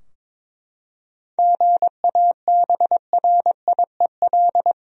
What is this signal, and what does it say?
Morse code, heard as a keyed tone, about 21 wpm, 700 Hz. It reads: GABRIEL
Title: USA Common Names - 22wpm